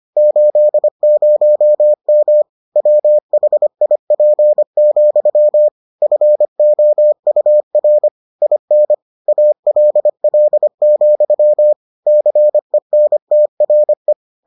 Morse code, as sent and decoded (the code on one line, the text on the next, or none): ---.. ----- -- / .-- .... .. .--. --..-- / ..-. --- ..- .-. / .. -. / .- .-.. .-.. --..-- / -.-. -. - .-. .
80M WHIP, FOUR IN ALL, CENTRE